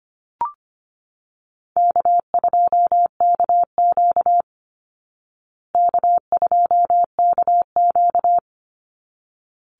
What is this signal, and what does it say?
Morse code, heard as a keyed tone, about 25 wpm, 700 Hz. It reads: K2KQ K2KQ
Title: Easy Call Signs Repeated 2x - 25wpm